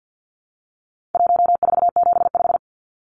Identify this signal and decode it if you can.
Morse code, heard as a keyed tone, about 50 wpm, 700 Hz. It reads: 1475
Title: Numbers - 4-Digit 50wpm